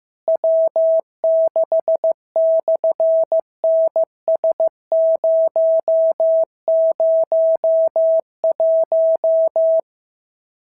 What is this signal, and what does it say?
Morse code, heard as a keyed tone, about 15 wpm, 650 Hz. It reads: W6/NS001
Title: SOTA QSO Practice 15wpm